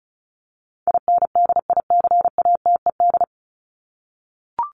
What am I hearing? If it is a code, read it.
Morse code, heard as a keyed tone, about 34 wpm, 700 Hz. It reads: INDICATED
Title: ED Suffix - Top 100 Words with Base Words 35wpm